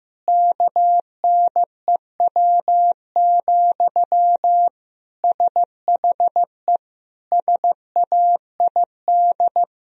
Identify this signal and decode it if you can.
Morse code, heard as a keyed tone, about 15 wpm, 700 Hz. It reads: KNEW, SHE SAID
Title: Sentences from Top 1000 Words - Review Part 4 - 15wpm